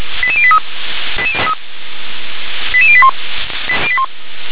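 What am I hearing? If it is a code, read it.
Five-tone selective call (ZVEI), heard as quick runs of five tones, tones of about 70 ms. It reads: 88083 88081
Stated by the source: ZVEI five-tone selective call 88640, received by SDR.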